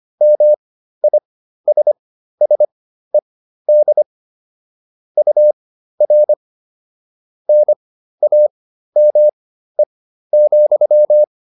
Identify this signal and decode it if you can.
Morse code, heard as a keyed tone, about 25 wpm, spaced as for 13 wpm, 600 Hz. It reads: MISSED UR NAME,